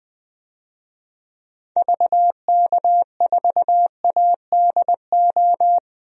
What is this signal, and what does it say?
Morse code, heard as a keyed tone, about 20 wpm, 700 Hz. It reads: VK4ADO
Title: DXCC-100 Call Signs - 6 Characters - 20wpm